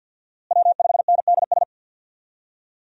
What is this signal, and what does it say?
Morse code, heard as a keyed tone, about 53 wpm, 700 Hz. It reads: W5NDS
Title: USA Call Signs - 5 Characters - Repeated 2x - 50wpm